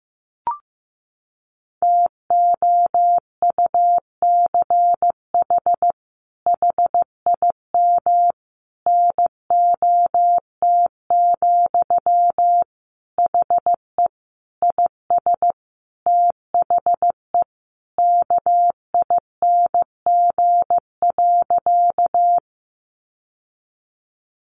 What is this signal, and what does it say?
Morse code, heard as a keyed tone, about 15 wpm, 700 Hz. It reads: TOUCH HIM NOT, HE IS THE KING.
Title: Sentences from Top 1200 Words 15wpm